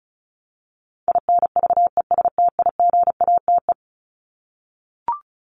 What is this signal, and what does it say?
Morse code, heard as a keyed tone, about 34 wpm, 700 Hz. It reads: INVESTIGATE
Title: Top 1751-2000 Words 35wpm